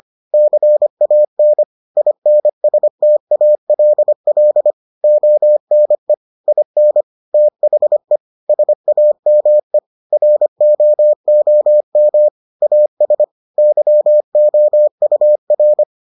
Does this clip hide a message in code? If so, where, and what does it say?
Yes: Morse code from 0.3 to 15.8 s, heard as a keyed tone, about 25 wpm, 600 Hz. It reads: CAN INSTALL ONE IN THE SAME ROOM AS YOUR